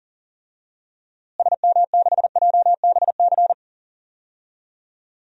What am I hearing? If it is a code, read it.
Morse code, heard as a keyed tone, about 40 wpm, 700 Hz. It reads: SM6JBC
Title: DXCC-100 Call Signs - 6 Characters - Repeated 2x - 40wpm